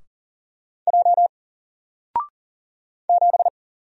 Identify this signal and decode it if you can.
Morse code, heard as a keyed tone, about 40 wpm, 700 Hz. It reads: J 7
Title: Single Letter-Number Alt 40wpm